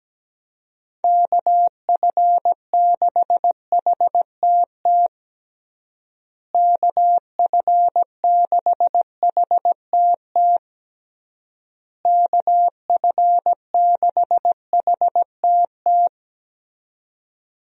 Morse code, heard as a keyed tone, about 17 wpm, 700 Hz. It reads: KF6HTT KF6HTT KF6HTT
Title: USA Call Signs - 6 Characters - Repeated 3x - 17wpm